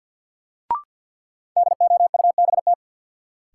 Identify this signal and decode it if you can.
Morse code, heard as a keyed tone, about 53 wpm, 700 Hz. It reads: DOUBT
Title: Sets of 5 Words - New Word Drill - 50wpm